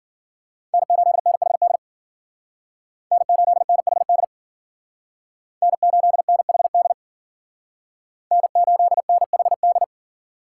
Morse code, heard as a keyed tone, about 51 wpm, 700 Hz. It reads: N8NHD N8NHD N8NHD N8NHD